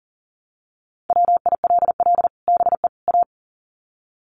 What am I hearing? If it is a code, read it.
Morse code, heard as a keyed tone, about 40 wpm, 700 Hz. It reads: WILL BE A
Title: Sets of 3 Words 40wpm